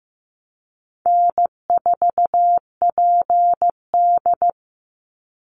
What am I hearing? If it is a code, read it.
Morse code, heard as a keyed tone, about 15 wpm, 700 Hz. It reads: N4PD
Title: Call Signs Repeated 2x - 15wpm